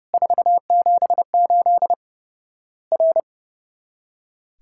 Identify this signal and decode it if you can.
Morse code, heard as a keyed tone, about 30 wpm, 700 Hz. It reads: 478 R
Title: Worked All Europe DX Contest - QTC Exchange - 30wpm